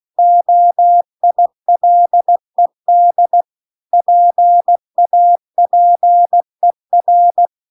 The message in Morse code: OILED PAPER

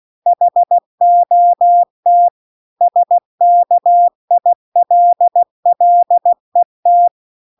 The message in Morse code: HOT SKILLET

